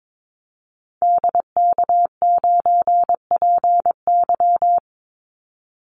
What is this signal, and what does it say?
Morse code, heard as a keyed tone, about 22 wpm, 700 Hz. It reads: DK9PY
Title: Call Signs - 22wpm